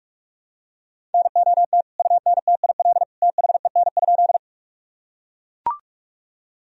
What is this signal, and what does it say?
Morse code, heard as a keyed tone, about 44 wpm, 700 Hz. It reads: NOT UNTIL THEN?
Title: Sentences from Top 1000 Words - Review Part 1 - 45wpm